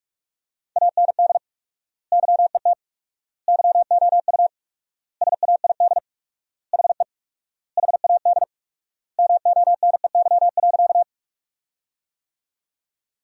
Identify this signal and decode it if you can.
Morse code, heard as a keyed tone, about 45 wpm, 700 Hz. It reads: AND YET YOU SAID HE HAD MONEY.
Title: 4x Word-Spacing - Sentences from Top 500 Words - 45wpm